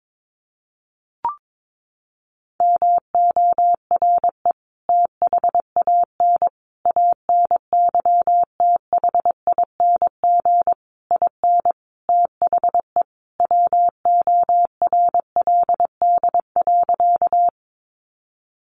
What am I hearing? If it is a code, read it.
Morse code, heard as a keyed tone, about 22 wpm, 700 Hz. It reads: MORE THAN ANYTHING IN THE WORLD.
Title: Sentences from Top 1000 Words 22wpm